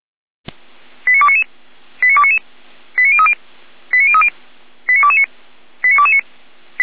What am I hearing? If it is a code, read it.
Five-tone selective call (ZVEI), heard as quick runs of five tones, tones of about 80 ms. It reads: 89299 89299 89030 89030 89229 89229
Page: /455/SDR/ZVEI/89173/